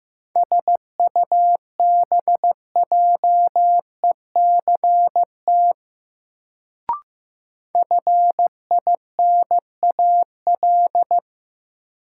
Morse code, heard as a keyed tone, about 15 wpm, 700 Hz. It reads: SUBJECT FINAL
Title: Sentences from Top 800 Words - New Word Drill - 15wpm